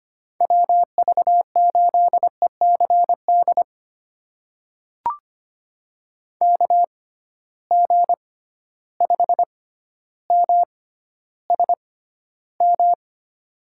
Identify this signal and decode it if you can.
Morse code, heard as a keyed tone, about 25 wpm, spaced as for 13 wpm, 700 Hz. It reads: WV8ECD KG5MSM
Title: USA Call Signs - 6 Characters - 6x Character Spacing - 25wpm